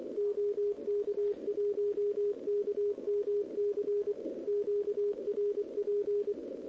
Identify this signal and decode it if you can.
Morse code, heard as a keyed tone, about 24 wpm, 410 Hz, with radio noise. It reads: OK1KMC QRP
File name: OK QRP 2026 / OKQRP2026